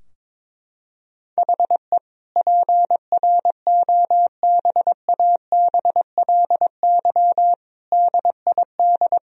Morse code, heard as a keyed tone, about 21 wpm, 700 Hz. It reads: HE PROBABLY DID
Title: Sentences from Top 1000 Words - Review Part 4 - 22wpm